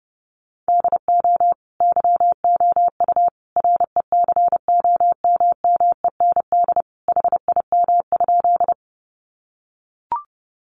Morse code, heard as a keyed tone, about 30 wpm, 700 Hz. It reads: DO YOU RECOMMEND HIM?
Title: Sentences from Top 1200 Words 30wpm